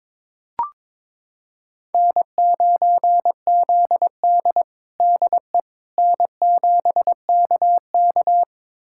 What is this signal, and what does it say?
Morse code, heard as a keyed tone, about 22 wpm, 700 Hz. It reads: N9ZD DE N7KK
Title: USA Callsign DE Callsign - 4 Characters - 22wpm